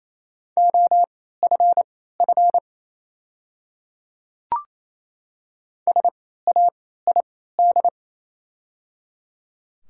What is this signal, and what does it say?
Morse code, heard as a keyed tone, about 28 wpm, spaced as for 16 wpm, 700 Hz. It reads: OFF SAID
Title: 3x Character Spacing - Top 300 Words - 28wpm